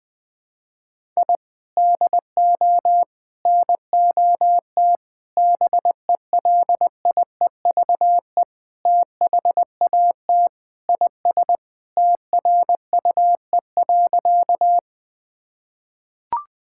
Morse code, heard as a keyed tone, about 20 wpm, 700 Hz. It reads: I DO NOT BELIEVE THAT IS TRUE.
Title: Sentences with 4-Word Phrases - 20wpm